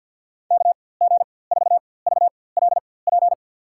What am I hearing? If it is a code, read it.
Morse code, heard as a keyed tone, about 51 wpm, spaced as for 23 wpm, 700 Hz. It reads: KG4VLP
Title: USA Call Signs - 6 Characters - 4x Character Spacing - 50wpm